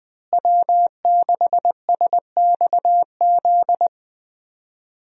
Morse code, heard as a keyed tone, about 20 wpm, 700 Hz. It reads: W6SXZ
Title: USA Call Signs - 5 Characters - Repeated 3x - 20wpm